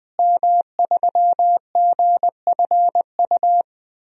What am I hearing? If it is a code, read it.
Morse code, heard as a keyed tone, about 21 wpm, 700 Hz. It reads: M3GFU
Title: DXCC-100 Call Signs - 5 Characters - Repeated 2x - 20wpm